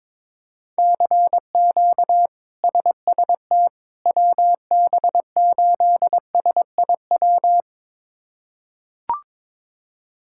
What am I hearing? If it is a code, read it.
Morse code, heard as a keyed tone, about 22 wpm, 700 Hz. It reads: CQ SST WB8SIW
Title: K1USN SST QSO Practice - 22wpm